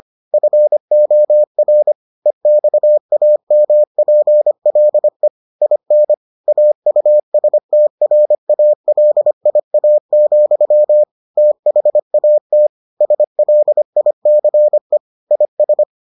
Morse code, heard as a keyed tone, about 25 wpm, 600 Hz. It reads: FOR EXAMPLE IN AUSTRALIA, THAT SLICE IS